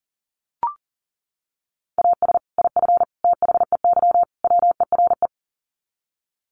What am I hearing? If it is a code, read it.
Morse code, heard as a keyed tone, about 40 wpm, 700 Hz. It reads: AS IF THEY WERE
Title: Sets of 4 Words 40wpm